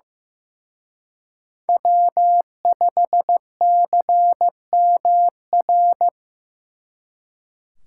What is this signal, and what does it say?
Morse code, heard as a keyed tone, about 18 wpm, 700 Hz. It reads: W5CMR
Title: Speed Racing - USA 1x3 Call Signs 15wpm